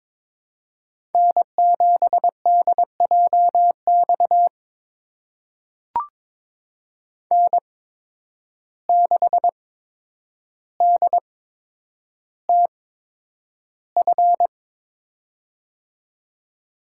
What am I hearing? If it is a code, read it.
Morse code, heard as a keyed tone, about 22 wpm, spaced as for 9 wpm, 700 Hz. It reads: N7DJX N6DTF
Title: USA Call Signs - 8x Character Spacing - 22wpm